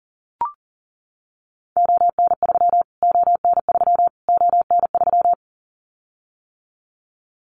Morse code, heard as a keyed tone, about 40 wpm, 700 Hz. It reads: ON3 ON3 ON3
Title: Callsign Prefixes for Top 25 DXCC - Repeated 3x - 40wpm